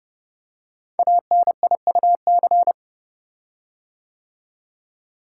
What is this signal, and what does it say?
Morse code, heard as a keyed tone, about 30 wpm, 700 Hz. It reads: ANIUC
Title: Code Groups - 5 Letter:Numbers - 30wpm